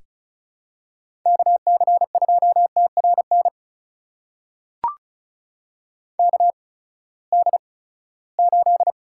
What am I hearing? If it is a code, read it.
Morse code, heard as a keyed tone, about 36 wpm, spaced as for 20 wpm, 700 Hz. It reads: KC2TRN KD8
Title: USA Call Signs - 6 Characters - 8x Character Spacing - 35wpm